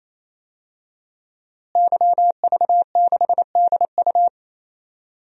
Morse code, heard as a keyed tone, about 28 wpm, 700 Hz. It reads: YV6DU
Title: DXCC-100 Call Signs - 5 Characters - Repeated 3x - 28wpm